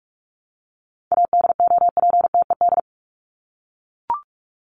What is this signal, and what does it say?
Morse code, heard as a keyed tone, about 44 wpm, 700 Hz. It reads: ADOPTED
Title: ED Suffix - Top 100 Words with Base Words 45wpm